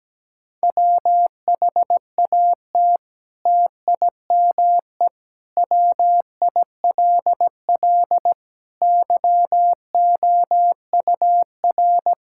Morse code, heard as a keyed tone, about 17 wpm, 700 Hz. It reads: WHAT TIME WILL YOUR
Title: Sentences from Top 1000 Words - Review Part 5 - 17wpm